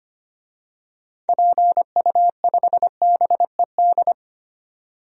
Morse code, heard as a keyed tone, about 25 wpm, 700 Hz. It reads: PU5BED
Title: DXCC-100 Call Signs - 6 Characters - 25wpm